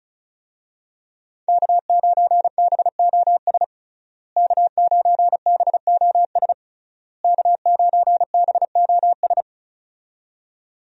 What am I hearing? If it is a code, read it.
Morse code, heard as a keyed tone, about 34 wpm, 700 Hz. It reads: K9BOS K9BOS K9BOS
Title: USA Call Signs - 5 Characters - Repeated 3x - 35wpm